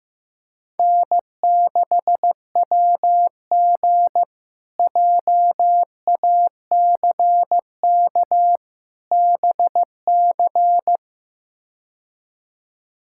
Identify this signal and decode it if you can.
Morse code, heard as a keyed tone, about 15 wpm, 700 Hz. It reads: N6WG JACK BC
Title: K1USN SST QSO Practice - 15wpm